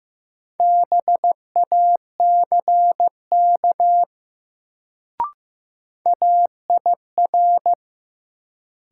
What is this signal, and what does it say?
Morse code, heard as a keyed tone, about 16 wpm, 700 Hz. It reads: BACK AIR
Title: Sentences from Top 200 Words - New Word Drill - 15wpm